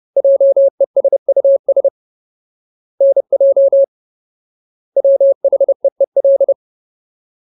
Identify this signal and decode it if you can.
Morse code, heard as a keyed tone, about 30 wpm, 550 Hz. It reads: JESUS NJ WHEEL
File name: Ultimate Word List / 30WPM Ultimate